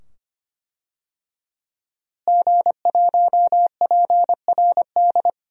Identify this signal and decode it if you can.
Morse code, heard as a keyed tone, about 25 wpm, 700 Hz. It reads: G1PRD
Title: DXCC-100 Call Signs - 5 Characters - Repeated 2x - 25wpm